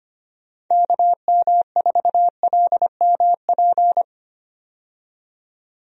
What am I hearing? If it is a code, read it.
Morse code, heard as a keyed tone, about 25 wpm, 700 Hz. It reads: KM4LMP
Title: USA Call Signs - 6 Characters - 25wpm